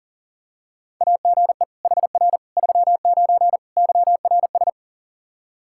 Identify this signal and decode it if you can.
Morse code, heard as a keyed tone, about 37 wpm, 700 Hz. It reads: AGE HR 39 YRS
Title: Practice QSO - Exchange 3 - More Info - 40wpm